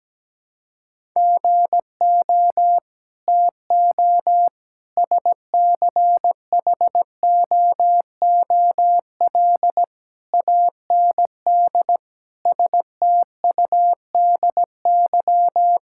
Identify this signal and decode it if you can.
Morse code, heard as a keyed tone, about 17 wpm, 700 Hz. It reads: GO TO SCHOOL AND STUDY
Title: Sentences from Top 1000 Words - Review Part 6 - 17wpm